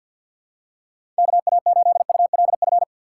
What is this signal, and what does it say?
Morse code, heard as a keyed tone, about 48 wpm, 700 Hz. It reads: KA8ULF
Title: USA Call Signs - 6 Characters - 4x Character Spacing - 50wpm